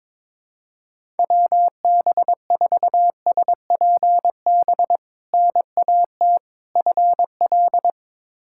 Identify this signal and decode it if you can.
Morse code, heard as a keyed tone, about 22 wpm, 700 Hz. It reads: WB4SPB NAT FL
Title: K1USN SST QSO Practice - 22wpm